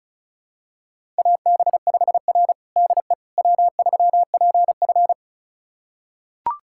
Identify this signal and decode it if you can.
Morse code, heard as a keyed tone, about 36 wpm, 700 Hz. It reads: AB5R DE W3PF
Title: USA Callsign DE Callsign - 4 Characters - 35wpm